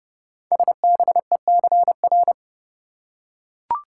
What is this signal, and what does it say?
Morse code, heard as a keyed tone, about 28 wpm, 700 Hz. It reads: SBECR
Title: Code Groups - 5 Letters - 30wpm